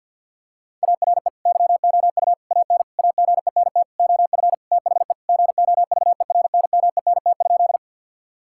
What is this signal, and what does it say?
Morse code, heard as a keyed tone, about 50 wpm, 700 Hz. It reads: ARE YOU AN AGENT OF THE GOVERNMENT?
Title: Sentences from Top 1600 Words 50wpm